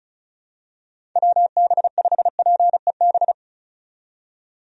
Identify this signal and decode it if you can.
Morse code, heard as a keyed tone, about 36 wpm, 700 Hz. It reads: WB5PEB
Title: USA Call Signs - 6 Characters - 35wpm